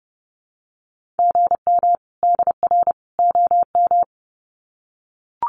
Morse code, heard as a keyed tone, about 30 wpm, 700 Hz. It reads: GM DR OM
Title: Common QSO Phrases 30wpm